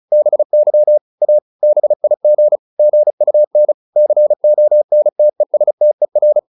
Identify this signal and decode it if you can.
Morse code, heard as a keyed tone, about 34 wpm, 600 Hz. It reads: BY A BIG GUN CONTESTER